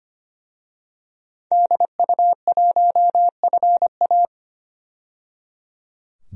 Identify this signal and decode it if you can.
Morse code, heard as a keyed tone, about 25 wpm, 700 Hz. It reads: DU1FA
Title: DXCC-100 Call Signs - 5 Characters - 25wpm